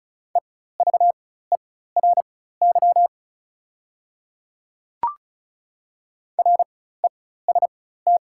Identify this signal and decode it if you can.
Morse code, heard as a keyed tone, about 36 wpm, spaced as for 16 wpm, 690 Hz. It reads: EVERY REST
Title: 4x Character Spacing - Top 500 Words - 35wpm